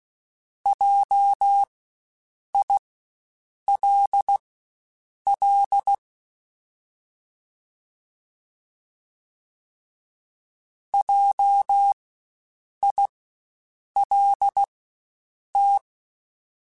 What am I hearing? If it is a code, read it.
Morse code, heard as a keyed tone, about 16 wpm, spaced as for 7 wpm, 800 Hz. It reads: JILL JILT
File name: Spaced Alphas / Spaced Alphas mp3s / Js sp @15